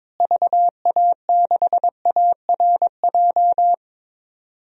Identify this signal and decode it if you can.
Morse code, heard as a keyed tone, about 22 wpm, 700 Hz. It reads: VA6ARJ
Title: DXCC-100 Call Signs - 6 Characters - Repeated 2x - 22wpm